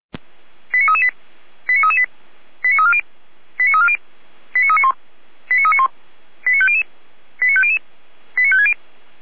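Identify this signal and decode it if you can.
Five-tone selective call (ZVEI), heard as quick runs of five tones, tones of about 70 ms. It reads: 89338 89338 89340 89340 89391 89391 89500 89500 89560
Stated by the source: Five-tone ZVEI selective call received on an SDR.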